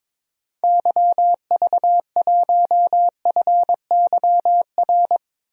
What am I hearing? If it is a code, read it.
Morse code, heard as a keyed tone, about 22 wpm, 700 Hz. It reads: YV1FYR
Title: DXCC-100 Call Signs - 6 Characters - 22wpm